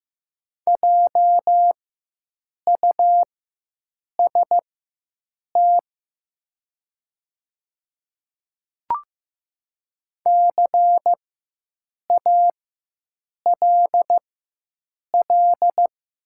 Morse code, heard as a keyed tone, about 15 wpm, spaced as for 7 wpm, 700 Hz. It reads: JUST CALL
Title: 4x Character Spacing - Top 300 Words - 15wpm